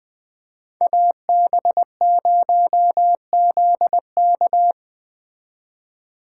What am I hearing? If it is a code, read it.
Morse code, heard as a keyed tone, about 20 wpm, 700 Hz. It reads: AB0ZK
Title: Speed Racing - USA 2x2 Call Signs 17wpm